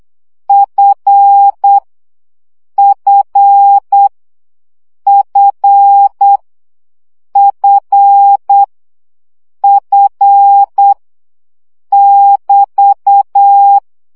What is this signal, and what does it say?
Morse code, heard as a keyed tone, about 8 wpm, 800 Hz. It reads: F F F F F <BT>
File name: CW gyakorlas f 7 baud